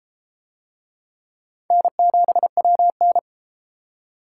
Morse code, heard as a keyed tone, about 31 wpm, 700 Hz. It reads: N7WN